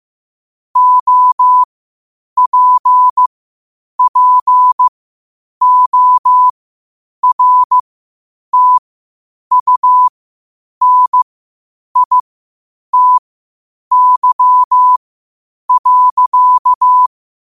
Morse code, heard as a keyed tone, about 15 wpm, spaced as for 8 wpm, 1000 Hz. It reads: OPPORTUNITY.
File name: 8wpm